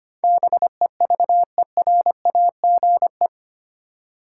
Morse code, heard as a keyed tone, about 25 wpm, 700 Hz. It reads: BEVERAGE
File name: Eight Letter Words - 25wpm